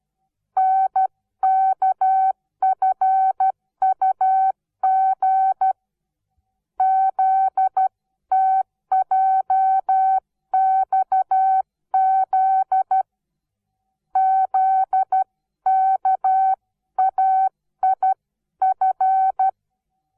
Morse code, heard as a keyed tone, about 12 wpm, 760 Hz. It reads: NKFUG ZTJXZ ZKAIF
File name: Onlinelehrgang / morsen / DJ4UF Morsen17